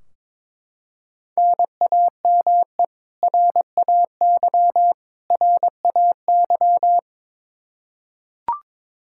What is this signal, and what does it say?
Morse code, heard as a keyed tone, about 21 wpm, 700 Hz. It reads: NAME RAY RAY
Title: Practice QSO - Exchange 1 - Beginning - 22wpm